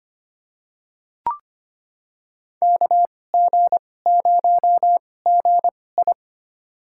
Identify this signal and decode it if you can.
Morse code, heard as a keyed tone, about 25 wpm, spaced as for 18 wpm, 700 Hz. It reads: KG0GI
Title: USA Call Signs - 5 Characters - 2x Character Spacing - 25wpm